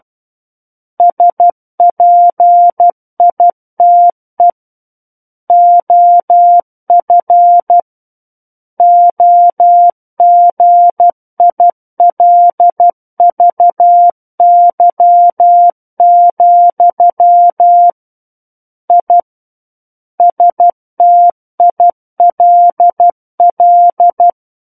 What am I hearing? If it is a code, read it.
Morse code, heard as a keyed tone, about 12 wpm, 700 Hz. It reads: SPITE OF OGILVY, I STILL
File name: Book One / 03-Chapter 3 - 12 WPM